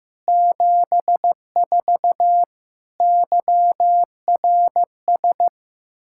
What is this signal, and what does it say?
Morse code, heard as a keyed tone, about 16 wpm, 700 Hz. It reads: 74 YRS
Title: Practice QSO - Exchange 3 - More Info - 15wpm